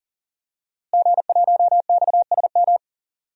Morse code, heard as a keyed tone, about 40 wpm, 700 Hz. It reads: G1XSM
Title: DXCC-100 Call Signs - 5 Characters - Repeated 2x - 40wpm